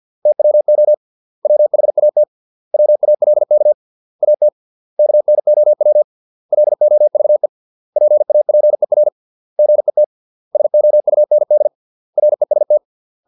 Morse code, heard as a keyed tone, about 50 wpm, 600 Hz. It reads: TWO WHAT WALK AT KNOW LOVE PAPER GET SOUND REST